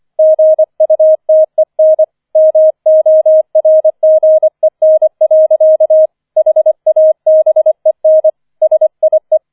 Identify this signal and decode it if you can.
Morse code, heard as a keyed tone, about 25 wpm, 630 Hz. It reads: GUTEN MORGEN. HABEN SIE
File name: 005 Fünfter Streich 25wpm